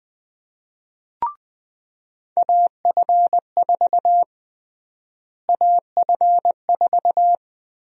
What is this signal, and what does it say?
Morse code, heard as a keyed tone, about 20 wpm, 700 Hz. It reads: AF4 AF4
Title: USA Call Sign Prefixes - Repeated 2x - 20wpm